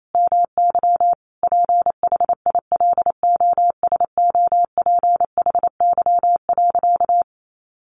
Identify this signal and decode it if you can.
Morse code, heard as a keyed tone, about 28 wpm, 700 Hz. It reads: MY PHILOSOPHY.
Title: Sentences from Top 900 Words 28wpm